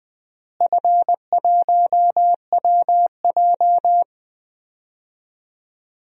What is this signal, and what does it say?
Morse code, heard as a keyed tone, about 21 wpm, 700 Hz. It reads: F1WJ
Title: DXCC-100 Call Signs - 4 Characters - Repeated 3x - 20wpm